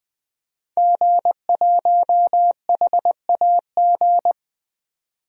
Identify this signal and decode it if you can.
Morse code, heard as a keyed tone, about 20 wpm, 700 Hz. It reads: G1HAG
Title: DXCC-100 Call Signs - 5 Characters - 20wpm